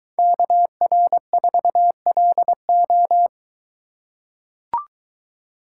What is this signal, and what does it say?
Morse code, heard as a keyed tone, about 23 wpm, 700 Hz. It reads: KR4LO